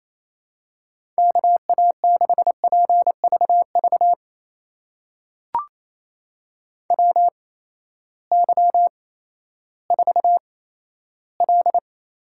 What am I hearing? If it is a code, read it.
Morse code, heard as a keyed tone, about 28 wpm, spaced as for 14 wpm, 700 Hz. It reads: KA6PVV WY4L
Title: USA Call Signs - 8x Character Spacing - 28wpm